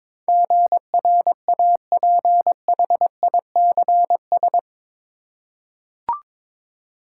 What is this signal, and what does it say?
Morse code, heard as a keyed tone, about 21 wpm, 700 Hz. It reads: GRAPHICS
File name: practice / 22wpm / Eight Letter Words - 22wpm